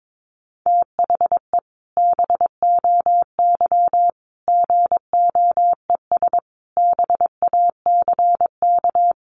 Morse code, heard as a keyed tone, about 22 wpm, 700 Hz. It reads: THE BOY GOES BACK
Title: Additional Easy Phrases 22wpm